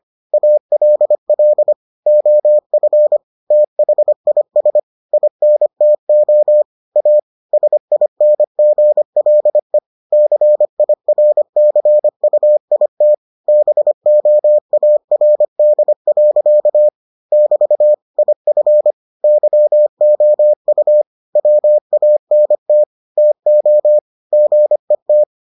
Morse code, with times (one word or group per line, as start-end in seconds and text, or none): 0.3-1.7 ALL
2.1-3.2 OF
3.5-4.8 THIS
5.1-6.6 INTO
7.0-7.2 A
7.5-9.8 SINGLE
10.1-13.1 CIRCUIT
13.5-16.9 BOARD.
17.3-17.9 <BT>
18.2-18.9 IF
19.2-21.0 YOU
21.4-22.8 WANT
23.2-24.0 TO
24.3-25.2 GET